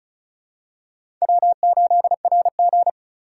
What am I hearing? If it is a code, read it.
Morse code, heard as a keyed tone, about 36 wpm, 700 Hz. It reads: W8RG
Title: USA Call Signs - 4 Characters - 2x Character Spacing - 35wpm